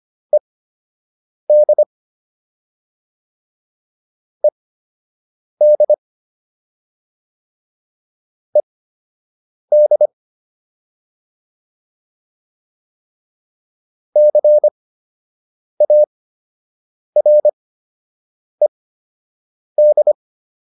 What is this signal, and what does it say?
Morse code, heard as a keyed tone, about 25 wpm, spaced as for 7 wpm, 600 Hz. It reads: ED ED ED CARED